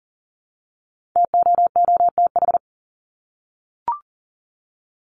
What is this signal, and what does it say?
Morse code, heard as a keyed tone, about 40 wpm, 700 Hz. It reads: TOOTH
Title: Top 1751-2000 Words 40wpm